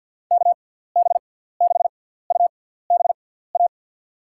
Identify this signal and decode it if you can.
Morse code, heard as a keyed tone, about 52 wpm, spaced as for 17 wpm, 700 Hz. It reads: KB6UBA
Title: USA Call Signs - 6x Character Spacing - 50wpm